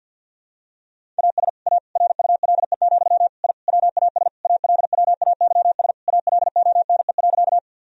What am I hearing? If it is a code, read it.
Morse code, heard as a keyed tone, about 50 wpm, 700 Hz. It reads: AS A RULE, I WAS ALWAYS ALONE.
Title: Sentences from Top 1000 Words - Review Part 2 - 50wpm